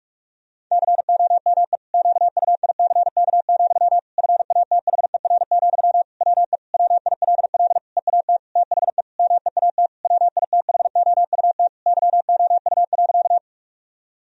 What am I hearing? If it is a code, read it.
Morse code, heard as a keyed tone, about 46 wpm, 700 Hz. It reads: COME QUICK, FATHER, WE WILL EAT THE MEAT WITHOUT YOU.
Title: Sentences from Top 1000 Words - Review Part 3 - 45wpm